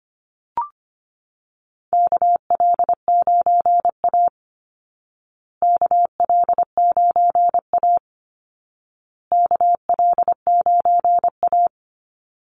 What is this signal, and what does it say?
Morse code, heard as a keyed tone, about 25 wpm, 700 Hz. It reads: KL9A KL9A KL9A
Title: Call Signs Repeated 2x - 25wpm